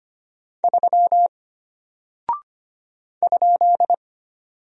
Morse code, heard as a keyed tone, about 25 wpm, 700 Hz. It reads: 3 ?
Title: Single Character (Letter:Number:Punctuation:Prosign) Alt 25wpm